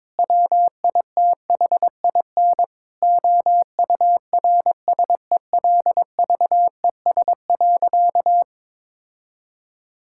Morse code, heard as a keyed tone, about 22 wpm, 700 Hz. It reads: WITHIN OURSELVES.
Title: Sentences from Top 1600 Words 22wpm